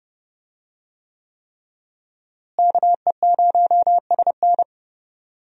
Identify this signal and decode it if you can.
Morse code, heard as a keyed tone, about 30 wpm, 700 Hz. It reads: KE0SN
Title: USA Call Signs - 5 Characters - Repeated 3x - 30wpm